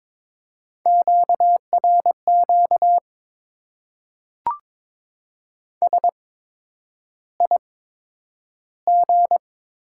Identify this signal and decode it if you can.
Morse code, heard as a keyed tone, about 21 wpm, spaced as for 9 wpm, 700 Hz. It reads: QRQ SIG
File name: Top 90 QSO Elements - 8x Character Spacing - 22wpm